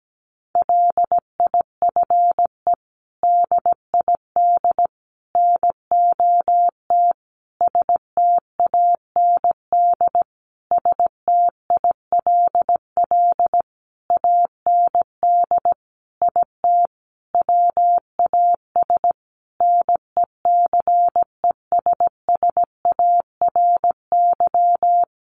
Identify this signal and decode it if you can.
Morse code, heard as a keyed tone, about 17 wpm, 700 Hz. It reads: LIFE DID NOT STAND STILL AND IT WAS NECESSARY
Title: Sentences from Top 800 Words 17wpm